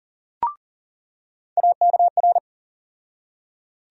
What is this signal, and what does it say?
Morse code, heard as a keyed tone, about 40 wpm, 700 Hz. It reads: AKR
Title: USA Call Sign Postfixes - 3 Letters 40wpm